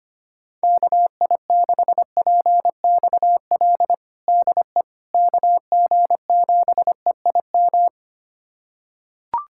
Morse code, heard as a keyed tone, about 25 wpm, 700 Hz. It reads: KI6PXL DE KG7EIM
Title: USA Callsign DE Callsign 25wpm